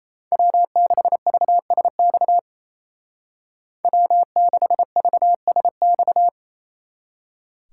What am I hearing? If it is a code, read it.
Morse code, heard as a keyed tone, about 36 wpm, 700 Hz. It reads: W6VSX W6VSX